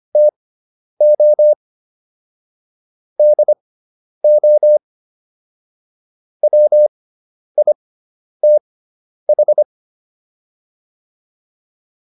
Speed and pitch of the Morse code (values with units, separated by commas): 25 wpm, spaced as for 10 wpm, 600 Hz